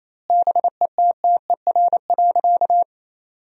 Morse code, heard as a keyed tone, about 28 wpm, 700 Hz. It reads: BETTER.
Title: 2x Word-Spacing - Sentences from Top 400 Words - 28wpm